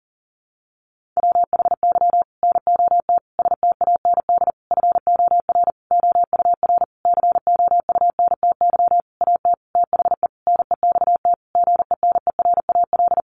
Morse code, heard as a keyed tone, about 40 wpm, 700 Hz. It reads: WHY NOT STAND FOR OUR COUNTY AT THE NEXT GENERAL
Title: Sentences from Top 1500 Words 40wpm